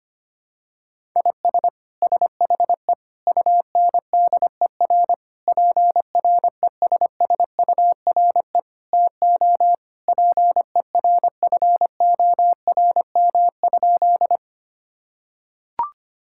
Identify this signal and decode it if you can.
Morse code, heard as a keyed tone, about 25 wpm, 700 Hz. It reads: IS SHE UNDER PRESSURE TO PERFORM?
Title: Sentences from Top 1000 Words - Review Part 6 - 25wpm